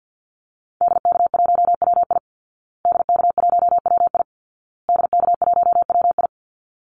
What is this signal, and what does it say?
Morse code, heard as a keyed tone, about 50 wpm, 700 Hz. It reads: DK1WI DK1WI DK1WI
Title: Call Signs Repeated 2x - 50wpm